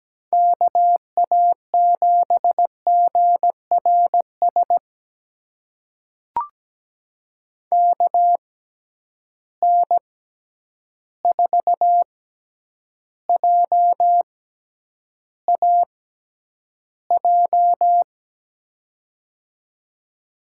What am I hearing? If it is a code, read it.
Morse code, heard as a keyed tone, about 17 wpm, spaced as for 9 wpm, 700 Hz. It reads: KA7GRS KN4JAJ
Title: USA Call Signs - 6x Character Spacing - 17wpm